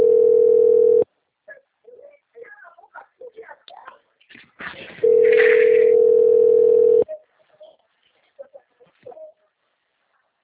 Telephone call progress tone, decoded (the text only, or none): ringback tone